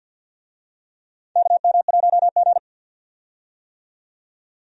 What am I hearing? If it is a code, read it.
Morse code, heard as a keyed tone, about 46 wpm, 700 Hz. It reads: KM1G
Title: USA Call Signs - 4 Characters - 50wpm